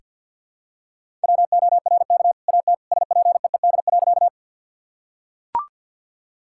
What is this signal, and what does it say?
Morse code, heard as a keyed tone, about 45 wpm, 700 Hz. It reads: WORK AT SPEED.
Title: Sentences from Top 1000 Words - Review Part 5 - 50wpm